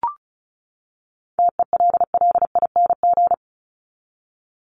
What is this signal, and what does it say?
Morse code, heard as a keyed tone, about 34 wpm, 700 Hz. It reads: TELLING
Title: ING Suffix - Top 100 Words with Base Words 35wpm